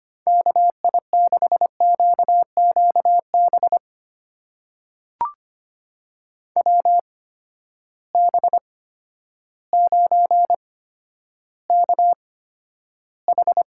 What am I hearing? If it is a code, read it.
Morse code, heard as a keyed tone, about 25 wpm, spaced as for 11 wpm, 700 Hz. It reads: KI6QQB WB9KH
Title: USA Call Signs - 6 Characters - 8x Character Spacing - 25wpm